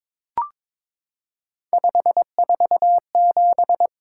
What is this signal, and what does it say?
Morse code, heard as a keyed tone, about 22 wpm, 700 Hz. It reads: 547
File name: practice / 22wpm / Three Letter-Number Combinations - 22wpm